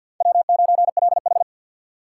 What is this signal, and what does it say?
Morse code, heard as a keyed tone, about 52 wpm, 700 Hz. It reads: W9LH
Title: USA Call Signs - 4 Characters - 2x Character Spacing - 50wpm